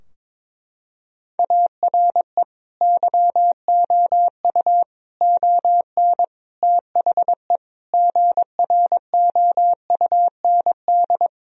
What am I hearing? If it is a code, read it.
Morse code, heard as a keyed tone, about 22 wpm, 700 Hz. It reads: ARE YOU ON THE GROUND
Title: Sentences from Top 1000 Words - Review Part 2 - 22wpm